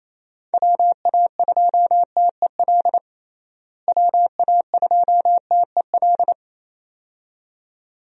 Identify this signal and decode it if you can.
Morse code, heard as a keyed tone, about 28 wpm, 700 Hz. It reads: WA2TEL WA2TEL
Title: USA Call Signs - 6 Characters - Repeated 2x - 28wpm